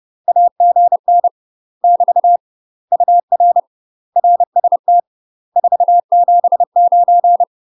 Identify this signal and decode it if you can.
Morse code, heard as a keyed tone, about 30 wpm, 700 Hz. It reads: AGN <BT> UR RST 479
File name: UK MASTER QSOs TRAINING / 30WPM / UK MASTER QSOs 0044